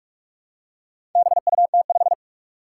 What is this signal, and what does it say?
Morse code, heard as a keyed tone, about 44 wpm, 700 Hz. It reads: BUT5
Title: Four Challenging Letter-Number Combinations 45wpm